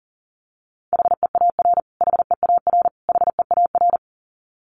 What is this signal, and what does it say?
Morse code, heard as a keyed tone, about 40 wpm, 700 Hz. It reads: HEAR HEAR HEAR
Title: Top 500 Words Repeated 40wpm